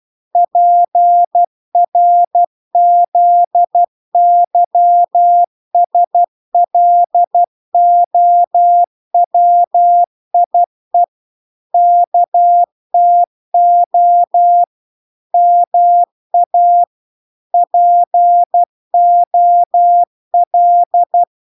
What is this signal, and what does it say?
Morse code, heard as a keyed tone, about 12 wpm, 700 Hz. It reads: PRZYSLOWIE KTO MA POL